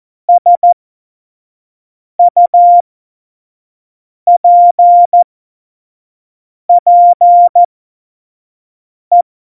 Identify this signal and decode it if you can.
Morse code, heard as a keyed tone, about 14 wpm, spaced as for 5 wpm, 700 Hz. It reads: SUPPE
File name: Morsekurs / Witze / Witz 19. November 2019, 25 F70